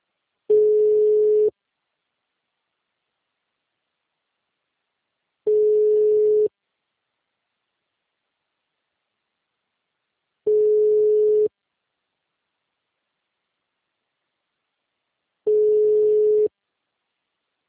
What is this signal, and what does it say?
Ringback tone (Europe, 425 Hz).
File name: out- -2017- - - .119